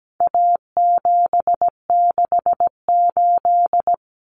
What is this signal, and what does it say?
Morse code, heard as a keyed tone, about 17 wpm, 700 Hz. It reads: A768
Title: Common Cut Numbers - 4-Digit 17wpm